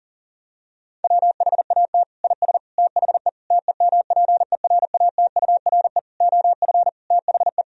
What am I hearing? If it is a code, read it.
Morse code, heard as a keyed tone, about 40 wpm, 700 Hz. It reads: WHAT IS THE TEMPERATURE OF THE
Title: Sentences from Top 1000 Words - Review Part 4 - 40wpm